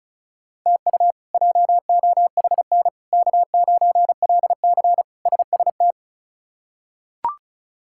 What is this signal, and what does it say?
Morse code, heard as a keyed tone, about 36 wpm, 700 Hz. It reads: TU JOHN K9LC SST
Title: K1USN SST QSO Practice - 35wpm